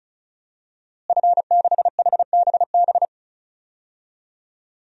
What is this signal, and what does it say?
Morse code, heard as a keyed tone, about 34 wpm, 700 Hz. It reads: F6HBB
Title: DXCC-100 Call Signs - 5 Characters - Repeated 2x - 35wpm